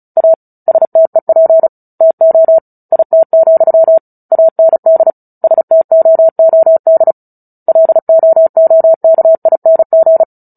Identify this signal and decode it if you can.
Morse code, heard as a keyed tone, about 34 wpm, 650 Hz. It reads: A STEP TO IT, AND STOOD LOOKING
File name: Book Two / 09-Chapter 9 - 35 WPM